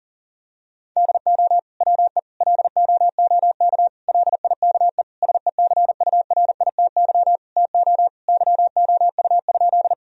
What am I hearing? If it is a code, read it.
Morse code, heard as a keyed tone, about 38 wpm, 700 Hz. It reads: DO WE LOOK LIKE SECURITY TO YOU?
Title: Sentences from Top 1000 Words - Review Part 4 - 40wpm